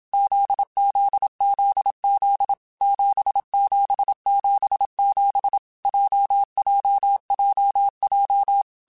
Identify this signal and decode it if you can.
Morse code, heard as a keyed tone, about 27 wpm, 800 Hz. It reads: ZZZZ 7777 JJJJ
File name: Groups@25 / Groups@25 mp3s / JZ17@25